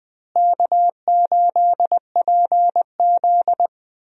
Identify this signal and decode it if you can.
Morse code, heard as a keyed tone, about 20 wpm, 700 Hz. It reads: K8PZ
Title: USA Call Signs - 4 Characters - Repeated 3x - 20wpm